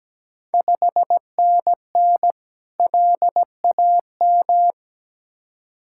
Morse code, heard as a keyed tone, about 17 wpm, 700 Hz. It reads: 5NN LAM
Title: Ontario QSO Party Exchange 17wpm